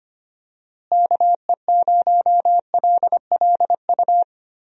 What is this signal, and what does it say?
Morse code, heard as a keyed tone, about 25 wpm, 700 Hz. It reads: KE0LLU
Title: USA Call Signs - 6 Characters - Repeated 3x - 25wpm